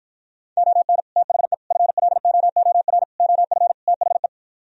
Morse code, heard as a keyed tone, about 52 wpm, 700 Hz. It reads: ON THE FLOOR OF THE